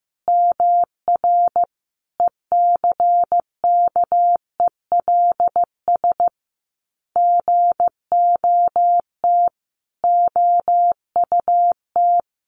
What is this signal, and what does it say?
Morse code, heard as a keyed tone, about 15 wpm, 700 Hz. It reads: MR ECKELS GOT OUT
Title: A Sound Of Thunder 15wpm